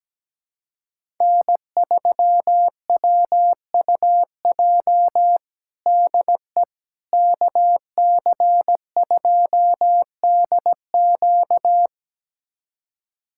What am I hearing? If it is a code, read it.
Morse code, heard as a keyed tone, about 17 wpm, 700 Hz. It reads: N3WUJ DE KC2DQ
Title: USA Callsign DE Callsign - 5 Characters - 17wpm